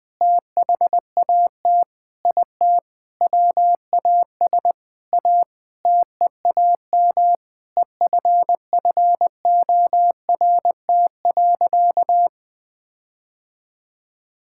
Morse code, heard as a keyed tone, about 20 wpm, 700 Hz. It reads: THAT IT WAS A TEAM EFFORT.
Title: Sentences with 8-Word Phrases - Spoken Beginning - 20wpm